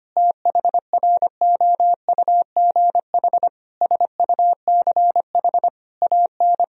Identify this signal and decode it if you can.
Morse code, heard as a keyed tone, about 25 wpm, 700 Hz. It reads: THROUGH SUCH AN
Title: Sentences from Top 1000 Words - Review Part 3 - 25wpm